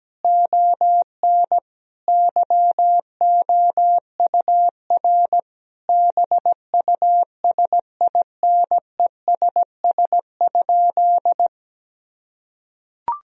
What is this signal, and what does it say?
Morse code, heard as a keyed tone, about 17 wpm, 700 Hz. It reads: ON YOUR BUSINESS?
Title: Sentences from Top 1000 Words - Review Part 4 - 17wpm